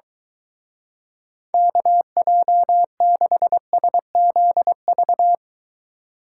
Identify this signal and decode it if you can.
Morse code, heard as a keyed tone, about 23 wpm, 700 Hz. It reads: KJ6SZV